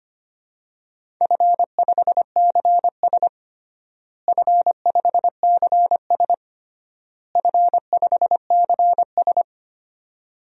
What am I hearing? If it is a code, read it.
Morse code, heard as a keyed tone, about 25 wpm, 700 Hz. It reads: F5CS F5CS F5CS
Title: DXCC-100 Call Signs - 4 Characters - Repeated 3x - 25wpm